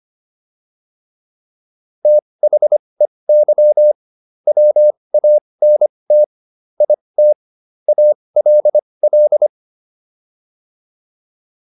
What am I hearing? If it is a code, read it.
Morse code, heard as a keyed tone, about 25 wpm, spaced as for 20 wpm, 600 Hz. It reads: THEY WANT IT ALL
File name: PR304 20